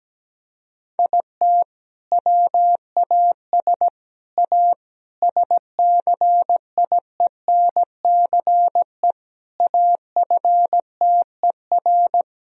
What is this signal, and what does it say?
Morse code, heard as a keyed tone, about 17 wpm, 700 Hz. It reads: IT WAS A SCIENCE AFTER
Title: Sentences from Top 1000 Words - Review Part 2 - 17wpm